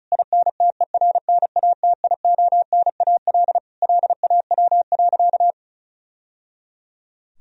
Morse code, heard as a keyed tone, about 36 wpm, 700 Hz. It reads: INTERNATIONAL LAW.
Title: Sentences from Top 1000 Words - Review Part 4 - 35wpm